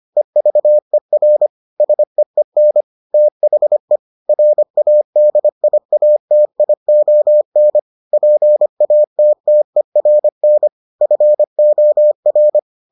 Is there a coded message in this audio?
Yes: Morse code from 0.2 to 12.6 s, heard as a keyed tone, about 25 wpm, 600 Hz. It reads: EVER SEEN THE RADIATION PATTERN FOR